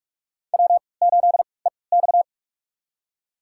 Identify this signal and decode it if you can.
Morse code, heard as a keyed tone, about 47 wpm, spaced as for 26 wpm, 700 Hz. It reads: W8EX
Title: USA Call Signs - 4 Characters - 3x Character Spacing - 45wpm